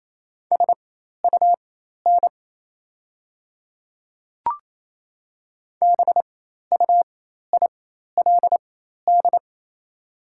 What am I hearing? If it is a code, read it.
Morse code, heard as a keyed tone, about 28 wpm, spaced as for 13 wpm, 700 Hz. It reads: SUN BUILD
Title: 4x Character Spacing - Top 300 Words - 28wpm